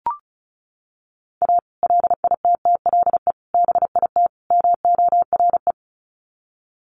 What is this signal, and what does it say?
Morse code, heard as a keyed tone, about 34 wpm, 700 Hz. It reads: A LITTLE BIT MORE
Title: Sets of 4 Words 35wpm